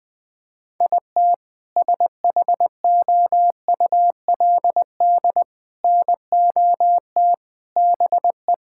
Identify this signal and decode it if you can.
Morse code, heard as a keyed tone, about 20 wpm, 700 Hz. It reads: IT SHOULD NOT BE
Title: Sentences with 2-Word Phrases - Spoken Beginning - 20wpm